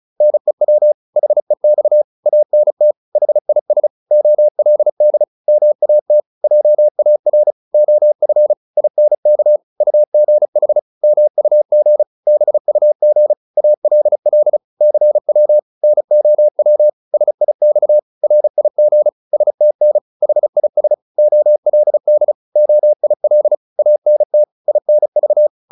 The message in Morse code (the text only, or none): NEW HEX ANT HIS OLD MAT JAR OF INK UGH MUG BUG ALL CW NOW SIX RIG STN HIS OLD OIL ANT INV